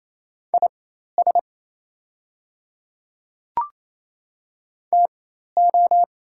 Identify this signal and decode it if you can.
Morse code, heard as a keyed tone, about 28 wpm, spaced as for 13 wpm, 700 Hz. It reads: IS TO
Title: 4x Character Spacing - Top 500 Words - 28wpm